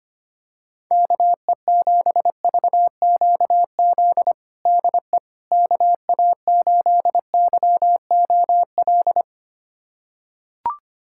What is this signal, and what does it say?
Morse code, heard as a keyed tone, about 25 wpm, 700 Hz. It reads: KE7VQZ DE KA8YOL
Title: USA Callsign DE Callsign - 6 Characters - 25wpm